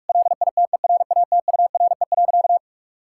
Morse code, heard as a keyed tone, about 46 wpm, 700 Hz. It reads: LITERATURE.
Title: Sentences from Top 1000 Words - Review Part 5 - 45wpm